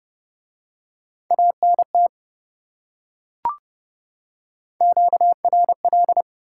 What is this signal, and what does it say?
Morse code, heard as a keyed tone, about 28 wpm, 700 Hz. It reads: ANT QRL